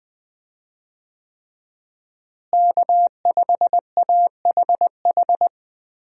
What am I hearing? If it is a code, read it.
Morse code, heard as a keyed tone, about 20 wpm, 700 Hz. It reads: K5AHH
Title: USA Call Signs - 5 Characters - Repeated 3x - 20wpm